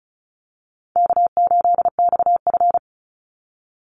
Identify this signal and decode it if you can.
Morse code, heard as a keyed tone, about 34 wpm, 700 Hz. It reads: K8XF
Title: Call Signs - 35wpm